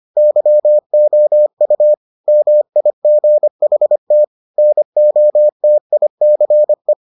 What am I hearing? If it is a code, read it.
Morse code, heard as a keyed tone, about 25 wpm, 600 Hz. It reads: YOU MIGHT NOTICE